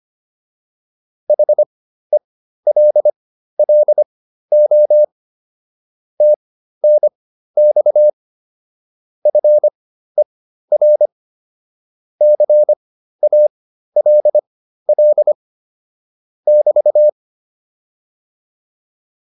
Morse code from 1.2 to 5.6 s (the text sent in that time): HELLO